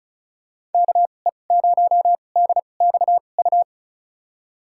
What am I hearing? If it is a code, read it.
Morse code, heard as a keyed tone, about 36 wpm, spaced as for 25 wpm, 700 Hz. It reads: KE0DXU
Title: USA Call Signs - 2x Character Spacing - 35wpm